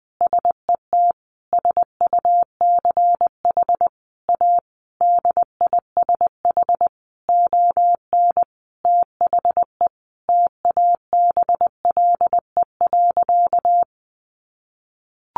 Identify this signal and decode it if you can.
Morse code, heard as a keyed tone, about 19 wpm, 700 Hz. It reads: SET SUCH A DISH ON THE TABLE.
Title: Sentences from Top 1300 Words 20wpm